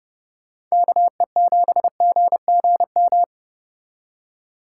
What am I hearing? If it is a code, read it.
Morse code, heard as a keyed tone, about 39 wpm, 700 Hz. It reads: KE7GGM